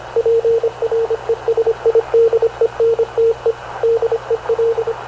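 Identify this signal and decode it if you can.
Morse code, heard as a keyed tone, about 25 wpm, 460 Hz, with radio noise. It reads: PRESIDENTE DEL
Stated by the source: Havana Radio CLA press (PX) broadcast, recorded by Radio Officer Paul Dumesnil aboard mv CNAV QUEST/CZDO.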